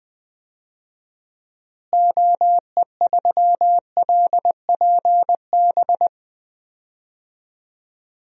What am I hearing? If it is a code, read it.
Morse code, heard as a keyed tone, about 20 wpm, 700 Hz. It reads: OE3LPB
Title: DXCC-100 Call Signs - 6 Characters - Repeated 2x - 20wpm